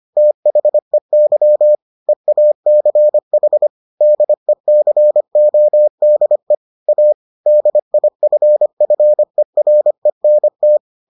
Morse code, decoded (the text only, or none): THEY EACH DECODE A DIFFERENT